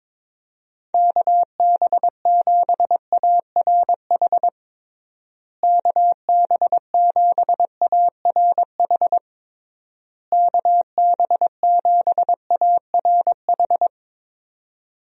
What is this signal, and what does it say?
Morse code, heard as a keyed tone, about 22 wpm, 700 Hz. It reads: KB7ARH KB7ARH KB7ARH
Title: USA Call Signs - 6 Characters - Repeated 3x - 22wpm